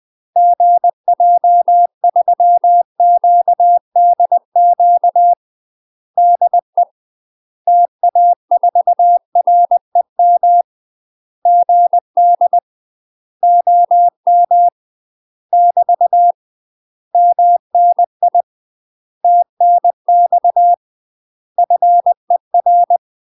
Morse code, heard as a keyed tone, about 20 wpm, 700 Hz. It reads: GJ3QDQ DE TA4REM GD OM <BT> MNI TNX FER
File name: UK MASTER QSOs TRAINING / 20WPM / UK MASTER QSOs 0071